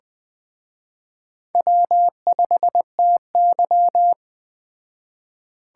Morse code, heard as a keyed tone, about 20 wpm, 700 Hz. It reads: W5TY
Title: Speed Racing - USA 1x2 Call Signs 17wpm